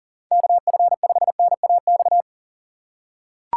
Morse code, heard as a keyed tone, about 40 wpm, 700 Hz. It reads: KF5NAX
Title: USA Call Signs - 6 Characters - 3x Character Spacing - 40wpm